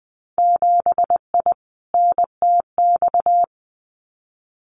Morse code, heard as a keyed tone, about 20 wpm, 700 Hz. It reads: 7I NTX
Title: Winter Field Day Exchanges Encore 20wpm